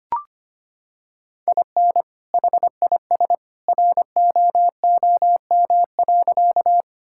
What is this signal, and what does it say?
Morse code, heard as a keyed tone, about 25 wpm, 700 Hz. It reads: IN HIS ROOM.
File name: practice / 25wpm / Sentences with 2-Word Phrases - 25wpm